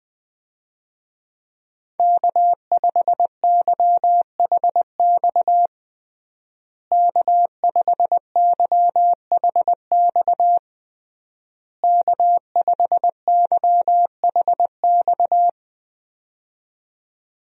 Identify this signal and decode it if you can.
Morse code, heard as a keyed tone, about 20 wpm, 700 Hz. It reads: K5YHX K5YHX K5YHX
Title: USA Call Signs - 5 Characters - Repeated 3x - 20wpm